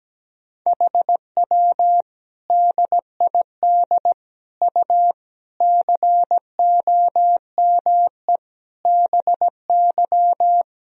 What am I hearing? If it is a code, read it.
Morse code, heard as a keyed tone, about 17 wpm, 700 Hz. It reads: HW DID U COME BY
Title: Sentences from Top 100 Words and Abbreviations 17wpm